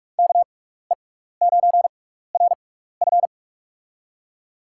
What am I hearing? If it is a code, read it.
Morse code, heard as a keyed tone, about 42 wpm, spaced as for 15 wpm, 690 Hz. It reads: KE9RF
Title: USA Call Signs - 5 Characters - 6x Character Spacing - 45wpm